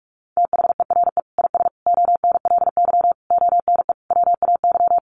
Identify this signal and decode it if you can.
Morse code, heard as a keyed tone, about 44 wpm, 700 Hz. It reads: THERE IS ONLY ONE WAY
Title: Sentences from Top 1200 Words 45wpm